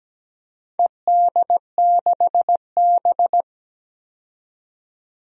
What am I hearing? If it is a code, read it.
Morse code, heard as a keyed tone, about 17 wpm, 700 Hz. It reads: ED6B
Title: DXCC-100 Call Signs - 4 Characters - 17wpm